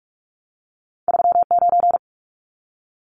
Morse code, heard as a keyed tone, about 46 wpm, 700 Hz. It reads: 39
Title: Numbers - 2-Digit 45wpm